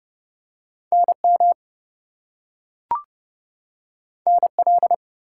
Morse code, heard as a keyed tone, about 30 wpm, 700 Hz. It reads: NM NL
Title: US States and Canadian Provinces 30wpm